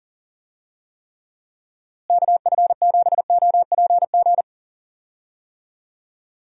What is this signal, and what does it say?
Morse code, heard as a keyed tone, about 40 wpm, 700 Hz. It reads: KF7OPG
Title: USA Call Signs - 6 Characters - Repeated 3x - 40wpm